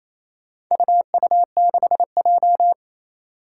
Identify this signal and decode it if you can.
Morse code, heard as a keyed tone, about 28 wpm, 700 Hz. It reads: UU6J
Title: Four Challenging Letter-Number Combinations 28wpm